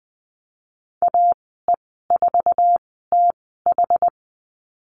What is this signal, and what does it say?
Morse code, heard as a keyed tone, about 20 wpm, 700 Hz. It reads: A E 4 T H
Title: 076 Lesson 13g - Letter L - Call Signs 20wpm